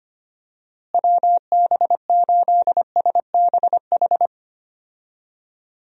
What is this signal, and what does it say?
Morse code, heard as a keyed tone, about 25 wpm, 700 Hz. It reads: WB8SBH
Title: USA Call Signs - 6 Characters - Repeated 3x - 25wpm